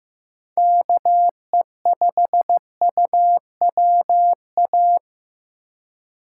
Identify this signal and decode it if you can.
Morse code, heard as a keyed tone, about 15 wpm, 700 Hz. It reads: KE5UWA
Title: USA Call Signs - 6 Characters - 3x Character Spacing - 15wpm